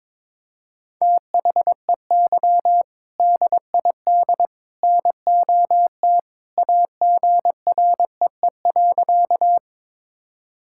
Morse code, heard as a keyed tone, about 22 wpm, 700 Hz. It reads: THEY DID NOT AGREE.
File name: practice / 22wpm / Sentences with 3-Word Phrases - 22wpm